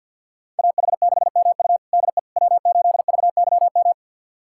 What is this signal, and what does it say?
Morse code, heard as a keyed tone, about 51 wpm, 700 Hz. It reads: AH6MU DE W8VYM
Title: USA Callsign DE Callsign - 5 Characters - 50wpm